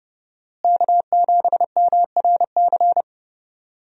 Morse code, heard as a keyed tone, about 32 wpm, 700 Hz. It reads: K7MRC
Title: USA Call Signs - 5 Characters - Repeated 2x - 30wpm